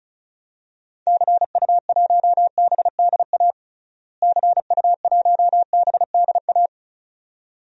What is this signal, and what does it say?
Morse code, heard as a keyed tone, about 36 wpm, 700 Hz. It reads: CU1BDA CU1BDA
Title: DXCC-100 Call Signs - 6 Characters - Repeated 2x - 35wpm